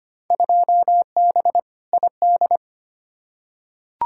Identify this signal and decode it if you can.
Morse code, heard as a keyed tone, about 25 wpm, 700 Hz. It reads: 2B ID
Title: Field Day Exchanges Encore - 25wpm